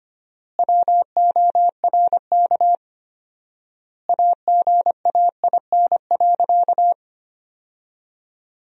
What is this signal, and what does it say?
Morse code, heard as a keyed tone, about 25 wpm, 700 Hz. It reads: WORK AGAIN.
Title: 4x Word-Spacing - Sentences from Top 200 Words - 25wpm